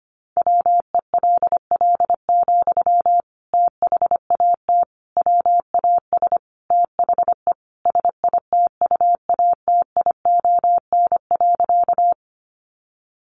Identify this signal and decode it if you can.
Morse code, heard as a keyed tone, about 25 wpm, 700 Hz. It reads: WELL, THAT WAS THE SITUATION.
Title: Sentences from Top 1200 Words 25wpm